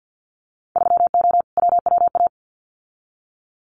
Morse code, heard as a keyed tone, about 50 wpm, 700 Hz. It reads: 3O WWA
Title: Winter Field Day Exchanges Encore 50wpm